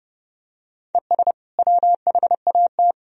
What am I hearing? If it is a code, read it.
Morse code, heard as a keyed tone, about 30 wpm, 700 Hz. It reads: ES WHAT
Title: Sentences from Top 100 Words and Abbreviations 30wpm